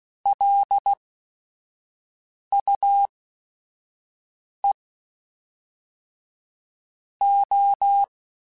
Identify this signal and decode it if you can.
Morse code, heard as a keyed tone, about 16 wpm, spaced as for 5 wpm, 800 Hz. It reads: LUE O
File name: Technical Subjects / Technical Subjects-mp3s / XC @5